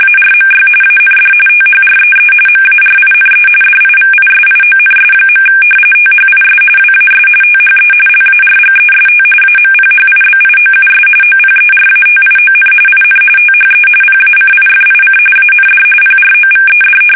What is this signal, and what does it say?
RTTY, about 45.45 baud. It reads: ARMY AND THREE MARINE DIVISIONS. THIS WAS THE LARGEST AMPHIBIOUS OPERATION IN THE PACIFIC AND PROVED
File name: fsk850 / battleshipnj